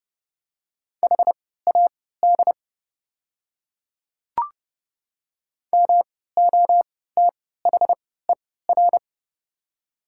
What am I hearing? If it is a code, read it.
Morse code, heard as a keyed tone, about 30 wpm, spaced as for 17 wpm, 700 Hz. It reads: HAD MOTHER
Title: 3x Character Spacing - Top 200 Words - 30wpm